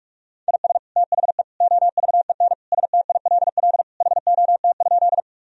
Morse code, heard as a keyed tone, about 41 wpm, 700 Hz. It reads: IS THE OVEN STILL HOT?
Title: Sentences from Top 1000 Words - Review Part 4 - 45wpm